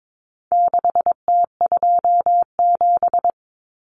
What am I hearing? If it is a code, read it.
Morse code, heard as a keyed tone, about 22 wpm, 700 Hz. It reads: 6T27
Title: Common Cut Numbers - 4-Digit 22wpm